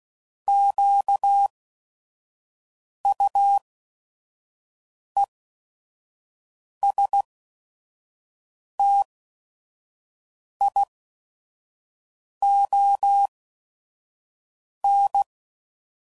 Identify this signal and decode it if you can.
Morse code, heard as a keyed tone, about 16 wpm, spaced as for 5 wpm, 800 Hz. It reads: QUESTION